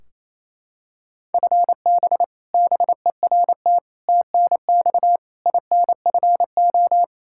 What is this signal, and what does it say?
Morse code, heard as a keyed tone, about 28 wpm, 700 Hz. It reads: FB BERT TNX INFO
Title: Practice QSO - Exchange 3 - More Info - 28wpm